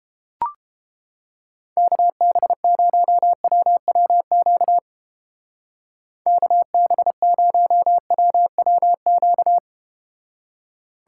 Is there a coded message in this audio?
Yes: Morse code from 1.8 to 9.6 s, heard as a keyed tone, about 33 wpm, 700 Hz. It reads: KB0WWQ KB0WWQ